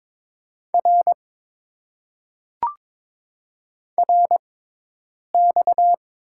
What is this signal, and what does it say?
Morse code, heard as a keyed tone, about 22 wpm, spaced as for 8 wpm, 700 Hz. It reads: R RX
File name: Top 90 QSO Elements - 6x Character Spacing - 22wpm